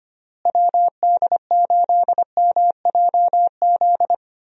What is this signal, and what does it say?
Morse code, heard as a keyed tone, about 25 wpm, 700 Hz. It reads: WD8MJZ
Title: USA Call Signs - 6 Characters - Repeated 3x - 25wpm